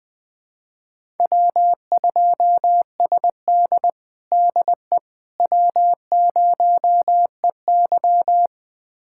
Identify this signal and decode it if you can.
Morse code, heard as a keyed tone, about 20 wpm, 700 Hz. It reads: W2SD DE W0EY
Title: USA Callsign DE Callsign - 4 Characters - 20wpm